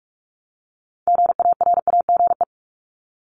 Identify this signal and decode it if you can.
Morse code, heard as a keyed tone, about 46 wpm, 700 Hz. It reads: GARAGE
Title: Multi-letter Phonemes Repeated 3x 45wpm